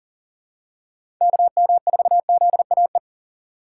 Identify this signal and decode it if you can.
Morse code, heard as a keyed tone, about 40 wpm, 700 Hz. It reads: KM4ZAE
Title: USA Call Signs - 6 Characters - 2x Character Spacing - 40wpm